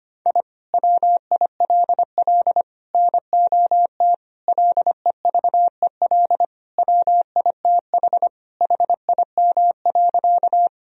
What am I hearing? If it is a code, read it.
Morse code, heard as a keyed tone, about 25 wpm, 700 Hz. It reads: I WILL NOT LEVEL WITH HIM.
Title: Sentences from Top 1000 Words - Review Part 4 - 25wpm